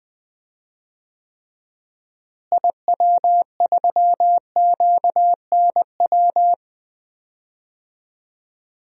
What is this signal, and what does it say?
Morse code, heard as a keyed tone, about 20 wpm, 700 Hz. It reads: IW3QNW
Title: DXCC-100 Call Signs - 6 Characters - Repeated 3x - 20wpm